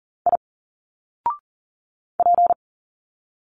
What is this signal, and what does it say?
Morse code, heard as a keyed tone, about 40 wpm, 700 Hz. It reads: I P
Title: Single Letters 40wpm